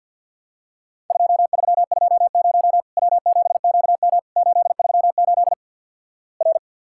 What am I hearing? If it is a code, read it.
Morse code, heard as a keyed tone, about 50 wpm, 700 Hz. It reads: 2320 W7QM 838 R
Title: Worked All Europe DX Contest - QTC Exchange - 50wpm